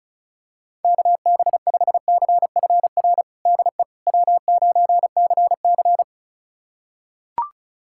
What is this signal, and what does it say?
Morse code, heard as a keyed tone, about 36 wpm, 700 Hz. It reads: KB5CFR DE W9CC
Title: USA Callsign DE Callsign 35wpm